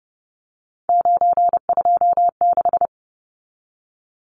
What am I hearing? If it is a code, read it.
Morse code, heard as a keyed tone, about 30 wpm, 700 Hz. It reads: 926
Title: Numbers - 3-Digit 30wpm